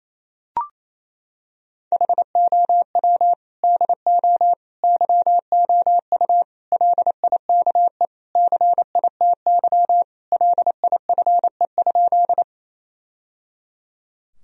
Morse code, heard as a keyed tone, about 28 wpm, 700 Hz. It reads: HOW DO YOU LIKE CITY LIFE?
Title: Sentences from Top 1000 Words - Review Part 1 - 28wpm